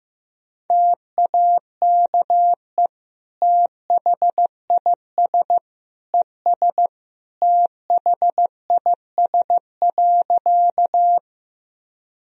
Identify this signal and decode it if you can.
Morse code, heard as a keyed tone, about 15 wpm, 700 Hz. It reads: TAKE THIS ES THIS.
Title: Sentences from Top 200 Words and Abbreviations 15wpm